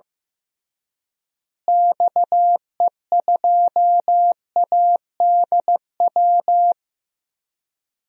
Morse code, heard as a keyed tone, about 16 wpm, 700 Hz. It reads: XE2ADW
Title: DXCC-100 Call Signs - 6 Characters - Repeated 3x - 15wpm